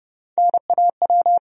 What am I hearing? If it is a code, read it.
Morse code, heard as a keyed tone, about 30 wpm, 700 Hz. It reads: NAW
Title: USA Call Sign Postfixes - 3 Letters 30wpm